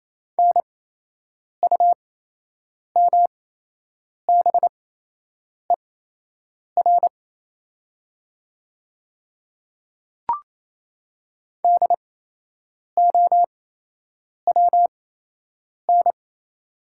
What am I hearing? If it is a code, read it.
Morse code, heard as a keyed tone, about 28 wpm, spaced as for 8 wpm, 700 Hz. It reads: NUMBER DOWN
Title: 8x Character Spacing - Top 100 Words - 28wpm